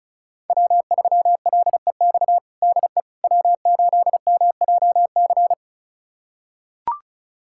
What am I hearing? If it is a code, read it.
Morse code, heard as a keyed tone, about 36 wpm, 700 Hz. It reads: W3LEX DE W8MJC
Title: USA Callsign DE Callsign - 5 Characters - 35wpm